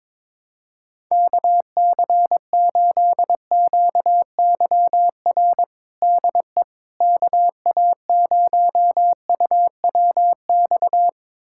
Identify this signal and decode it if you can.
Morse code, heard as a keyed tone, about 22 wpm, 700 Hz. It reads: KC8QYR DE KA0UWX
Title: USA Callsign DE Callsign - 6 Characters - 22wpm